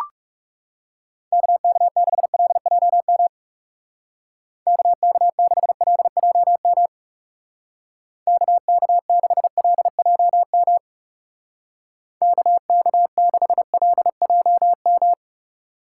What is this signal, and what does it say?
Morse code, heard as a keyed tone, about 37 wpm, 700 Hz. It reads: KK6LJM KK6LJM KK6LJM KK6LJM
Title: Speed Racing - USA 2x3 Call Signs 30wpm